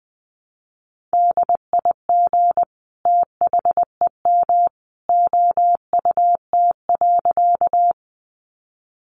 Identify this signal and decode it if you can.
Morse code, heard as a keyed tone, about 20 wpm, 700 Hz. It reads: DIG THEM OUT.
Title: A Sound Of Thunder 20wpm